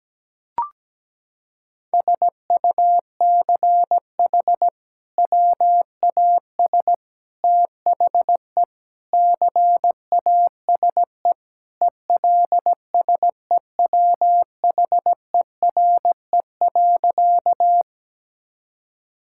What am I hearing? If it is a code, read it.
Morse code, heard as a keyed tone, about 17 wpm, 700 Hz. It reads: SUCH WAS THE CASE ELSEWHERE.
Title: Sentences from Top 1600 Words 17wpm